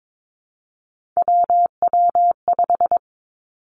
Morse code, heard as a keyed tone, about 22 wpm, 700 Hz. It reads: WW5
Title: Three Challenging Letter-Number Combinations 22wpm